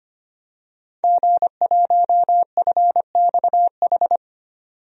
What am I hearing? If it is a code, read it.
Morse code, heard as a keyed tone, about 25 wpm, 700 Hz. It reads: G1FXH
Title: DXCC-100 Call Signs - 5 Characters - 25wpm